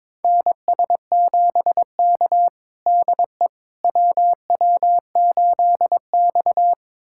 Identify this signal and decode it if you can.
Morse code, heard as a keyed tone, about 23 wpm, 700 Hz. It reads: NS7K DE WW8X
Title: USA Callsign DE Callsign - 4 Characters - 22wpm